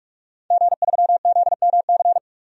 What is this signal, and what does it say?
Morse code, heard as a keyed tone, about 46 wpm, 700 Hz. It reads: G3ZMC
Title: DXCC-100 Call Signs - 5 Characters - 45wpm